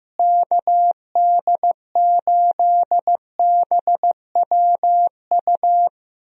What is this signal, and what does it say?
Morse code, heard as a keyed tone, about 15 wpm, 700 Hz. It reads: KD8BWU
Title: USA Call Signs - 6 Characters - Repeated 2x - 15wpm